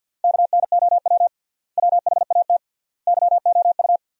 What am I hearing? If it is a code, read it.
Morse code, heard as a keyed tone, about 50 wpm, 700 Hz. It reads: KNOW WHAT YOU
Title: 3x Word-Spacing - Sentences from Top 100 Words - 50wpm